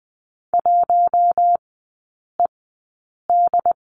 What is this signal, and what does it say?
Morse code, heard as a keyed tone, about 20 wpm, spaced as for 8 wpm, 700 Hz. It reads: 1ED
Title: 093 Lesson 16e - Letter c - Three Characters 20wpm